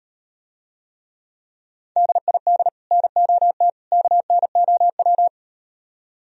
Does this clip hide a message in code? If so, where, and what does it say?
Yes: Morse code from 2.0 to 5.3 s, heard as a keyed tone, about 38 wpm, 700 Hz. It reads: DID NOT KNOW